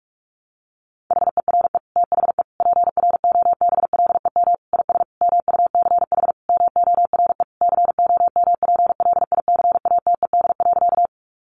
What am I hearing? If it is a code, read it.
Morse code, heard as a keyed tone, about 44 wpm, 700 Hz. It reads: HERE THE PROBLEM IS MUCH MORE COMPLICATED.
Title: Sentences from Top 1200 Words 45wpm